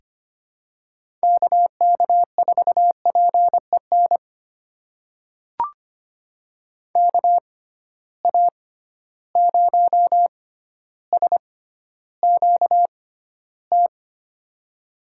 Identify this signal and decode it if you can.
Morse code, heard as a keyed tone, about 25 wpm, spaced as for 13 wpm, 700 Hz. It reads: KK4PEN KA0SQT
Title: USA Call Signs - 6 Characters - 6x Character Spacing - 25wpm